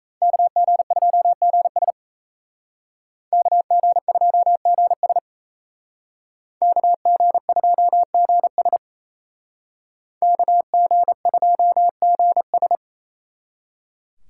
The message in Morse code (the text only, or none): KG2GS KG2GS KG2GS KG2GS